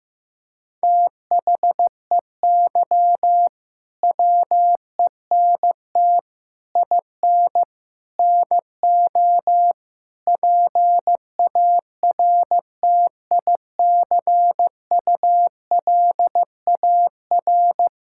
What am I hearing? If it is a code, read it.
Morse code, heard as a keyed tone, about 16 wpm, 700 Hz. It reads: THEY WENT IN NO PARTICULAR
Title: Sentences from Top 1000 Words - Review Part 5 - 15wpm